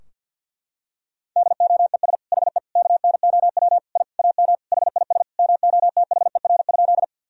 Morse code, heard as a keyed tone, about 52 wpm, 700 Hz. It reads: DOES HE KNOW I AM HIS MOTHER?
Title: Sentences from Top 1000 Words - Review Part 2 - 50wpm